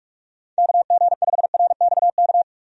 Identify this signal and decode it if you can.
Morse code, heard as a keyed tone, about 46 wpm, 700 Hz. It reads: KG5RXK
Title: USA Call Signs - 6 Characters - 45wpm